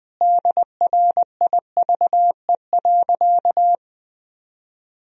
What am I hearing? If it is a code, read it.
Morse code, heard as a keyed tone, about 20 wpm, 700 Hz. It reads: DRIVE.
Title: Sentences from Top 1000 Words - Review Part 2 - 20wpm